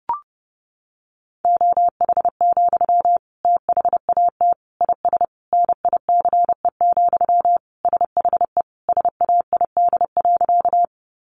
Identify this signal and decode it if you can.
Morse code, heard as a keyed tone, about 30 wpm, 700 Hz. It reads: OH, THAT IS NICE, SHE SAID.
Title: Sentences from Top 1100 Words 30wpm